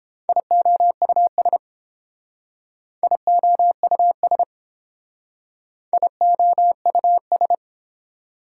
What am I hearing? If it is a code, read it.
Morse code, heard as a keyed tone, about 30 wpm, 700 Hz. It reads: IOUS IOUS IOUS